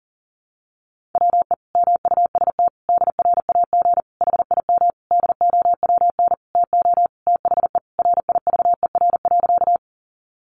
Morse code, heard as a keyed tone, about 40 wpm, 700 Hz. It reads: WE MUST DRAG HIM DOWN TO THE RIVER.
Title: Sentences from Top 1500 Words 40wpm